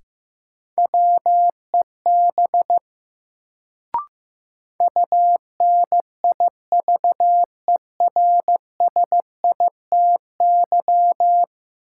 Morse code, heard as a keyed tone, about 15 wpm, 700 Hz. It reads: WEB UNIVERSITY
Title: Sentences from Top 1000 Words - Review Part 4 - New Word Drill - 15wpm